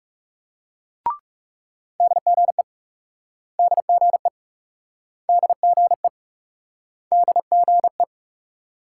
Morse code, heard as a keyed tone, about 39 wpm, 700 Hz. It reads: DGE DGE DGE DGE